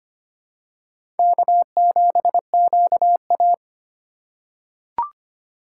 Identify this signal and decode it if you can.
Morse code, heard as a keyed tone, about 25 wpm, 700 Hz. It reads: K7QA
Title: USA Call Signs - 2x Character Spacing - 25wpm